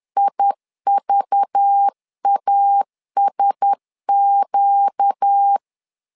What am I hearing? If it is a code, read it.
Morse code, heard as a keyed tone, about 10 wpm, 800 Hz. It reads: IVASQ
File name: LEZIONI / 5 LBVC / 5 LBV 10wpm 2248